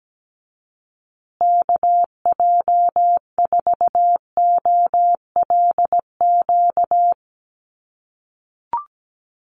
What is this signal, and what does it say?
Morse code, heard as a keyed tone, about 17 wpm, 700 Hz. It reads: KJ4OLQ
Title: USA Call Signs Repeated 2x 17wpm